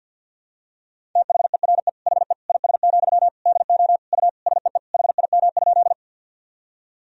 Morse code, heard as a keyed tone, about 52 wpm, 700 Hz. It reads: THERE HE IS, DO U SEE HIM?
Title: Sentences from Top 100 Words and Abbreviations 50wpm